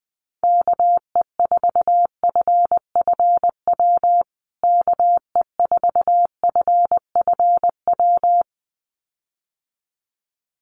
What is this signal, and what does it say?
Morse code, heard as a keyed tone, about 20 wpm, 700 Hz. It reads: KE4FFW KE4FFW
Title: USA Call Signs Repeated 2x 20wpm